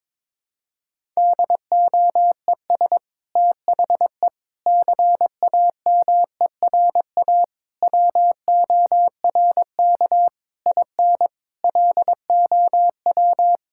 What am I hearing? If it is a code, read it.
Morse code, heard as a keyed tone, about 22 wpm, 700 Hz. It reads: DOES THE CAMERA WORK IN LOW
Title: Sentences from Top 1000 Words - Review Part 5 - 22wpm